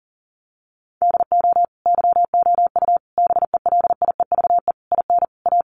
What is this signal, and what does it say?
Morse code, heard as a keyed tone, about 40 wpm, 700 Hz. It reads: DO YOU BELIEVE IN A
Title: Sentences from Top 1000 Words 40wpm